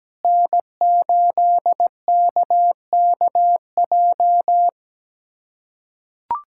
Morse code, heard as a keyed tone, about 17 wpm, 700 Hz. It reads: N8KKJ
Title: USA Callsign DE Callsign - 5 Characters - 17wpm